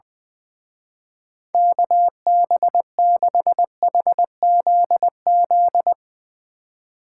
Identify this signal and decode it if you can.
Morse code, heard as a keyed tone, about 20 wpm, 700 Hz. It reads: KB6HZZ
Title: Speed Racing - USA 2x3 Call Signs 20wpm